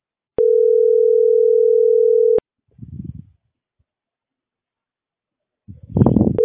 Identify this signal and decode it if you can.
Ringback tone (North America), 440 + 480 Hz.